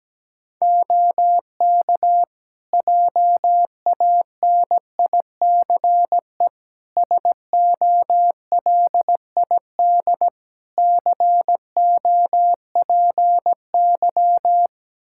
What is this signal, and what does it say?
Morse code, heard as a keyed tone, about 17 wpm, 700 Hz. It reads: OK JANICE SOLID COPY
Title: Practice QSO - Exchange 3 - More Info - 17wpm